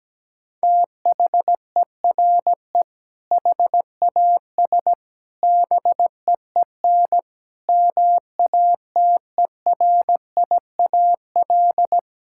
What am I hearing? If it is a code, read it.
Morse code, heard as a keyed tone, about 17 wpm, 700 Hz. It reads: THERE HAS BEEN MATERIAL
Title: Sentences from Top 1000 Words - Review Part 5 - 17wpm